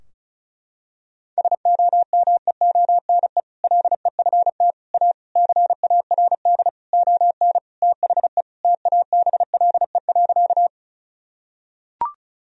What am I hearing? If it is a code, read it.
Morse code, heard as a keyed tone, about 34 wpm, 700 Hz. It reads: SOMEONE LEFT A CARD ON THE TABLE.
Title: Sentences from Top 1000 Words - Review Part 4 - 35wpm